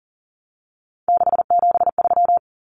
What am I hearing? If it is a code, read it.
Morse code, heard as a keyed tone, about 40 wpm, 700 Hz. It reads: 673
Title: Common Cut Numbers - 3-Digit 40wpm